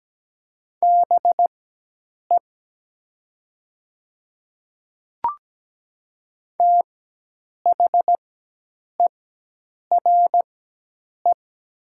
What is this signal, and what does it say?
Morse code, heard as a keyed tone, about 17 wpm, spaced as for 8 wpm, 700 Hz. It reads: BE THERE
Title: 4x Character Spacing - Top 100 Words - 17wpm